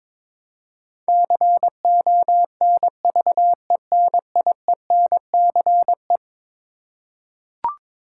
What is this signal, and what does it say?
Morse code, heard as a keyed tone, about 22 wpm, 700 Hz. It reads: CONVENIENCE
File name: practice / 22wpm / Eleven Letter Words - 22wpm